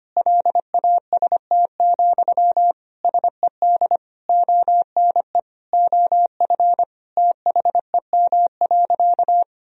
Morse code, heard as a keyed tone, about 25 wpm, 700 Hz. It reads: LAST, SED ONE OF THEM.
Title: Sentences from Top 300 Words and Abbreviations 25wpm